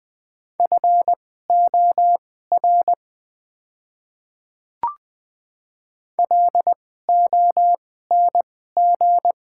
Morse code, heard as a keyed tone, about 20 wpm, spaced as for 14 wpm, 700 Hz. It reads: FOR LONG
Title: 2x Character Spacing - Top 300 Words - 20wpm